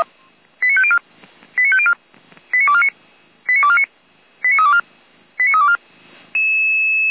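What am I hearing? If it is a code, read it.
Five-tone selective call (ZVEI), heard as quick runs of five tones, tones of about 80 ms. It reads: 89504 89504 89249 89249 89324 89324
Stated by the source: ZVEI five-tone call received via SDR, associated with Feuerwehr Orlamünde fire brigade.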